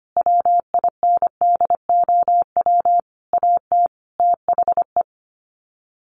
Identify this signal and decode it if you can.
Morse code, heard as a keyed tone, about 25 wpm, 700 Hz. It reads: WINDOW AT THE
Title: Sets of 5 Words Encore2 25wpm